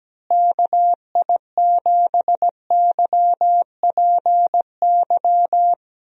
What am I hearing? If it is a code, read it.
Morse code, heard as a keyed tone, about 17 wpm, 700 Hz. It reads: KI7YPY
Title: USA Call Signs - 6 Characters - Repeated 2x - 17wpm